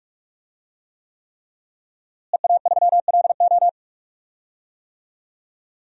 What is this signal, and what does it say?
Morse code, heard as a keyed tone, about 44 wpm, 700 Hz. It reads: EA3LO
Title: DXCC-100 Call Signs - 5 Characters - Repeated 3x - 45wpm